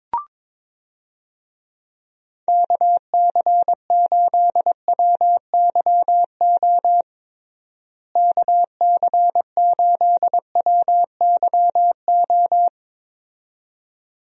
Morse code, heard as a keyed tone, about 22 wpm, 700 Hz. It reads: KC8WYO KC8WYO
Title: USA Call Signs - 6 Characters - Repeated 2x - 22wpm